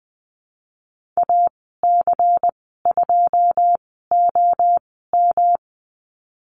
Morse code, heard as a keyed tone, about 20 wpm, spaced as for 14 wpm, 700 Hz. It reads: AC2OM
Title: 161 Lesson 26g - Letter g - Call Signs 20wpm